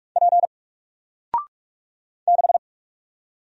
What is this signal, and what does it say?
Morse code, heard as a keyed tone, about 44 wpm, 700 Hz. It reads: P 6
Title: Single Character (Letter:Number:Punctuation:Prosign) 45wpm